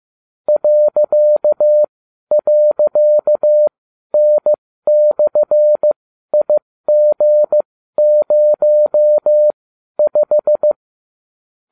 Morse code, heard as a keyed tone, about 15 wpm, spaced as for 12 wpm, 600 Hz. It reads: ..N/IG05